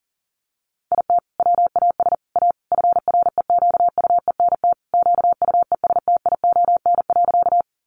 Morse code, heard as a keyed tone, about 37 wpm, 700 Hz. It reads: IT WAS A FREQUENT QUESTION.
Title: Sentences from Top 1400 Words 40wpm